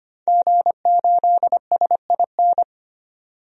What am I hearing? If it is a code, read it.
Morse code, heard as a keyed tone, about 25 wpm, 700 Hz. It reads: G8SIN
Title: DXCC-100 Call Signs - 5 Characters - 25wpm